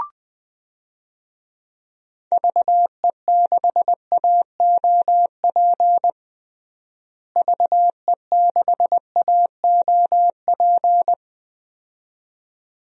Morse code, heard as a keyed tone, about 20 wpm, 700 Hz. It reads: VE6AOP VE6AOP
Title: DXCC-100 Call Signs - 6 Characters - Repeated 2x - 20wpm